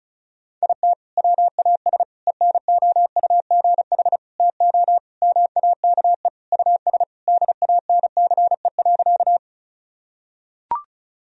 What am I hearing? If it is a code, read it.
Morse code, heard as a keyed tone, about 36 wpm, 700 Hz. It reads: IT WAS ENOUGH TO MAKE US DANCE.
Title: Sentences from Top 1000 Words - Review Part 4 - 35wpm